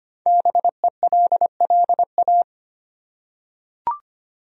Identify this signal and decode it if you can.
Morse code, heard as a keyed tone, about 25 wpm, 700 Hz. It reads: BELLA
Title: USA Common Names - 25wpm